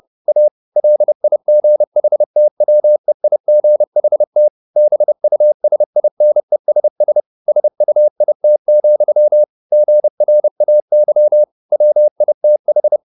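Morse code, heard as a keyed tone, about 30 wpm, 600 Hz. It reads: A LIGHTWEIGHT BUSINESS SUIT, GRAY WITH
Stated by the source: Morse code at 30 wpm, the novel Never by Ken Follett in English, from CW med Gnister.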